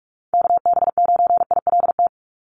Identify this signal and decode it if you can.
Morse code, heard as a keyed tone, about 44 wpm, 700 Hz. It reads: KB9ILT
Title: NA SOTA Activator Callsigns 45wpm